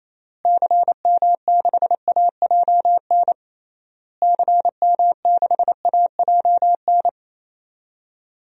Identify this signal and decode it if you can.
Morse code, heard as a keyed tone, about 28 wpm, 700 Hz. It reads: CM6AJN CM6AJN
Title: DXCC-100 Call Signs - 6 Characters - Repeated 3x - 28wpm